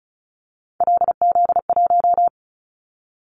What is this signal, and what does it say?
Morse code, heard as a keyed tone, about 36 wpm, 700 Hz. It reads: LZ1
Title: Callsign Prefixes for Top 50 DXCC - 35wpm